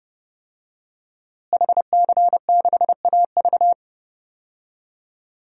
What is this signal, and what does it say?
Morse code, heard as a keyed tone, about 30 wpm, 700 Hz. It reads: HC6AV
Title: DXCC-100 Call Signs - 5 Characters - Repeated 2x - 30wpm